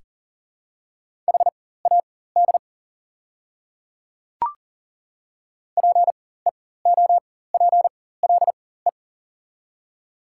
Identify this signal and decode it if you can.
Morse code, heard as a keyed tone, about 40 wpm, spaced as for 19 wpm, 700 Hz. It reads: HAD PEOPLE
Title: 4x Character Spacing - Top 200 Words - 40wpm